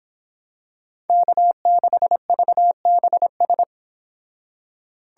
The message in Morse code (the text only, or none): K6VBS